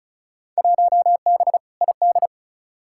Morse code, heard as a keyed tone, about 36 wpm, 700 Hz. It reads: 1B ID
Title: Field Day Exchanges - 35wpm